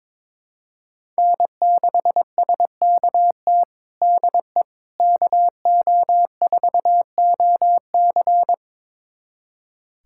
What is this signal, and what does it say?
Morse code, heard as a keyed tone, about 22 wpm, 700 Hz. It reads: N6SKT DE KO4OC
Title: USA Callsign DE Callsign - 5 Characters - 22wpm